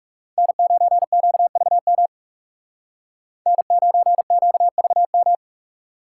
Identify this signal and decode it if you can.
Morse code, heard as a keyed tone, about 38 wpm, 700 Hz. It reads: N9QVM N9QVM